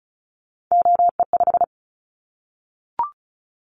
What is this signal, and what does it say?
Morse code, heard as a keyed tone, about 36 wpm, 700 Hz. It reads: OE5
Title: Callsign Prefixes for Top 100 DXCC - 35wpm